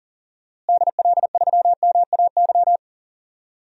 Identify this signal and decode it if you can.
Morse code, heard as a keyed tone, about 40 wpm, 700 Hz. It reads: DL3MAY
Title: DXCC-100 Call Signs - 6 Characters - Repeated 3x - 40wpm